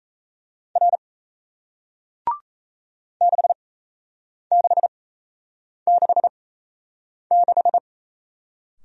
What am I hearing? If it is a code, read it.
Morse code, heard as a keyed tone, about 38 wpm, spaced as for 8 wpm, 700 Hz. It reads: R 6666